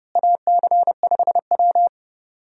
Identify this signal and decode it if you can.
Morse code, heard as a keyed tone, about 31 wpm, 700 Hz. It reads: AC5W
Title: USA Call Signs - 4 Characters - Repeated 2x - 30wpm